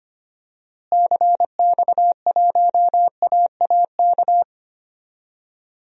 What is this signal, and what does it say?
Morse code, heard as a keyed tone, about 25 wpm, 700 Hz. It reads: CX1AAK
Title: DXCC-100 Call Signs - 6 Characters - Repeated 2x - 25wpm